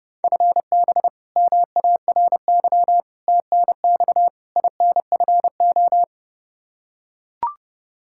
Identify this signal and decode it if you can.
Morse code, heard as a keyed tone, about 30 wpm, 700 Hz. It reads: FB MARY TNX INFO
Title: Practice QSO - Exchange 3 - More Info - 30wpm